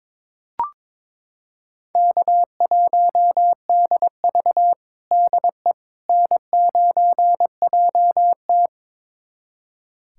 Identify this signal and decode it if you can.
Morse code, heard as a keyed tone, about 22 wpm, 700 Hz. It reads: K1DV DE N9JT
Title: USA Callsign DE Callsign - 4 Characters - 22wpm